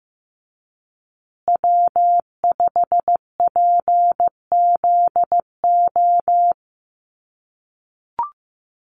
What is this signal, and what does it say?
Morse code, heard as a keyed tone, about 15 wpm, 700 Hz. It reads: W5PZO
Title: USA Call Signs Repeated 2x 15wpm